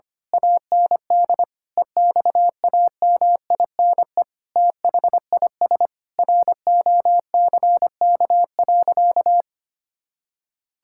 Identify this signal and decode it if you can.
Morse code, heard as a keyed tone, about 25 wpm, 700 Hz. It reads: AND EXAMINE THIS ROCK.
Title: Sentences from Top 1600 Words 25wpm